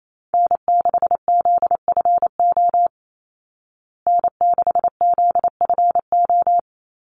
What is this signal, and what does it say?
Morse code, heard as a keyed tone, about 28 wpm, 690 Hz. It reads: N6ZFO N6ZFO
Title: Call Signs Repeated 3x - 28wpm